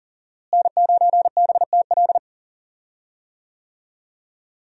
Morse code, heard as a keyed tone, about 39 wpm, 700 Hz. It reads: N9BTL
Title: USA Call Signs - 5 Characters - Repeated 3x - 40wpm